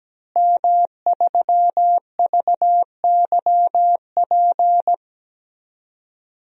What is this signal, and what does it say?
Morse code, heard as a keyed tone, about 17 wpm, 700 Hz. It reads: M3VYP
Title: DXCC-100 Call Signs - 5 Characters - Repeated 3x - 17wpm